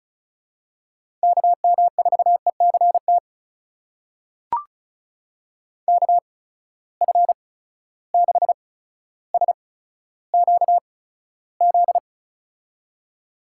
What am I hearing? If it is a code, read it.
Morse code, heard as a keyed tone, about 36 wpm, spaced as for 15 wpm, 700 Hz. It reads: KM4ECT KF6SQZ
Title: USA Call Signs - 6 Characters - 8x Character Spacing - 35wpm